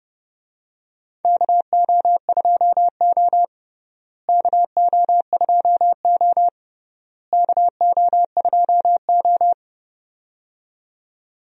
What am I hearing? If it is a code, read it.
Morse code, heard as a keyed tone, about 30 wpm, 700 Hz. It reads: KO2O KO2O KO2O
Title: USA Call Signs - 4 Characters - Repeated 3x - 30wpm